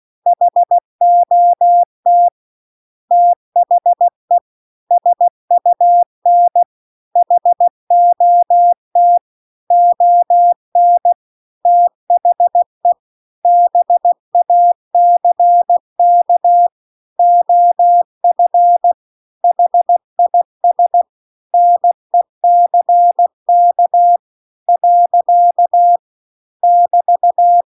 Morse code, heard as a keyed tone, about 16 wpm, 700 Hz. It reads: HOT THE SUN HOT ON THE BACK OF HIS NECK . <BT>